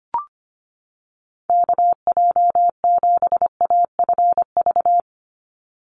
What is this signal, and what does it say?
Morse code, heard as a keyed tone, about 25 wpm, 700 Hz. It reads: KJ7AFV
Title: USA Call Signs 25wpm